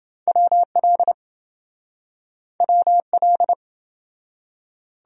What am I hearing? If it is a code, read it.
Morse code, heard as a keyed tone, about 26 wpm, 700 Hz. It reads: WL WL